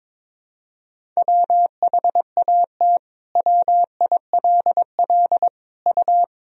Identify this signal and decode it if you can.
Morse code, heard as a keyed tone, about 22 wpm, 700 Hz. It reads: WHAT WILL U
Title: Sentences from Top 100 Words and Abbreviations 22wpm